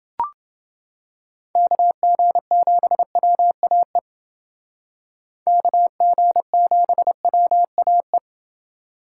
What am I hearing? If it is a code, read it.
Morse code, heard as a keyed tone, about 28 wpm, 700 Hz. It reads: KG7WAE KG7WAE